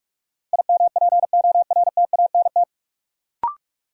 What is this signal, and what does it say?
Morse code, heard as a keyed tone, about 42 wpm, 700 Hz. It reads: IMPORTANT
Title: Sentences from Top 600 Words - New Word Drill - 45wpm